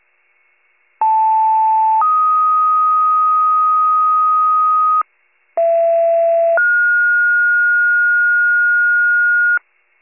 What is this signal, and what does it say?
Two-tone page, heard as a one-second tone, then a three-second tone, about 880 then 1250 Hz, about 670 then 1430 Hz, with noise. It is heard from 1.0 to 9.6 s.